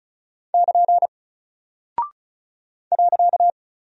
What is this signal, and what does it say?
Morse code, heard as a keyed tone, about 36 wpm, 700 Hz. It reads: <KN> .
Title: Single Character (Letter:Number:Punctuation:Prosign) Alt 35wpm